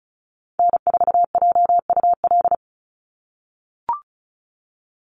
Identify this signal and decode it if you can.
Morse code, heard as a keyed tone, about 34 wpm, 700 Hz. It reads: N4JUL
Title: USA Call Signs 35wpm